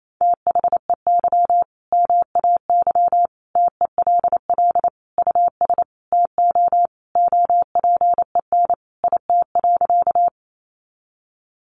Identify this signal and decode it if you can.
Morse code, heard as a keyed tone, about 28 wpm, 700 Hz. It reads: THEY MAY TELL US TO OPEN IT.
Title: Sentences from Top 300 Words 28wpm